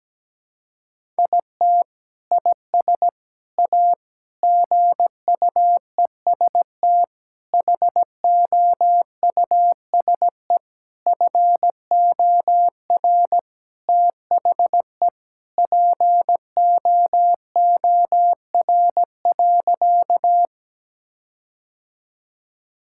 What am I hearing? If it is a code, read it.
Morse code, heard as a keyed tone, about 17 wpm, 700 Hz. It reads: IT IS A GUEST HOUSE FOR THE POOR.
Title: Sentences from Top 1600 Words 17wpm